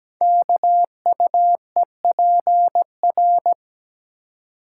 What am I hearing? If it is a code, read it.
Morse code, heard as a keyed tone, about 17 wpm, 700 Hz. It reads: KUEPR
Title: Code Groups - 5 Letters - 17wpm